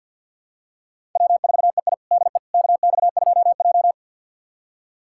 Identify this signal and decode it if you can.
Morse code, heard as a keyed tone, about 50 wpm, 700 Hz. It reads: W4EI DE KX2J
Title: USA Callsign DE Callsign - 4 Characters - 50wpm